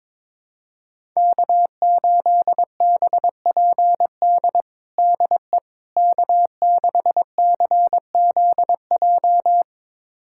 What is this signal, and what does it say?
Morse code, heard as a keyed tone, about 22 wpm, 700 Hz. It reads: K8BPD DE K6CZJ
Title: USA Callsign DE Callsign - 5 Characters - 22wpm